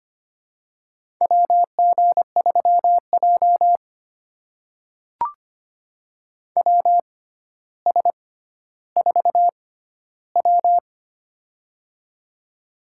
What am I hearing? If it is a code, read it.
Morse code, heard as a keyed tone, about 25 wpm, spaced as for 13 wpm, 700 Hz. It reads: WG3J WS4W
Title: USA Call Signs - 4 Characters - 6x Character Spacing - 25wpm